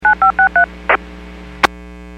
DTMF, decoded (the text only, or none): 9563